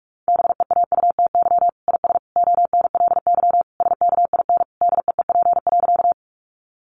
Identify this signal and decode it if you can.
Morse code, heard as a keyed tone, about 44 wpm, 700 Hz. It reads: BEAUTY IS ONLY SKIN DEEP.
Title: Sentences from Top 1300 Words 45wpm